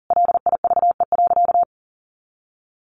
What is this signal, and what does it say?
Morse code, heard as a keyed tone, about 40 wpm, 700 Hz. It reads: LIVE.
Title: Sentences from Top 300 Words 40wpm